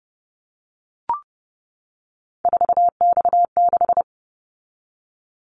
Morse code, heard as a keyed tone, about 30 wpm, 700 Hz. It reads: 4X6
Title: Callsign Prefixes for Top 50 DXCC - 30wpm